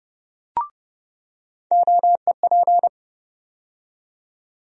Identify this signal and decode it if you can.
Morse code, heard as a keyed tone, about 32 wpm, 700 Hz. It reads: OEP
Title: USA Call Sign Postfixes - 3 Letters 30wpm